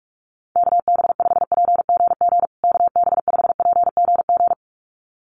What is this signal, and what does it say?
Morse code, heard as a keyed tone, about 44 wpm, 700 Hz. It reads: KB5PGG KB5PGG
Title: USA Call Signs Repeated 2x 45wpm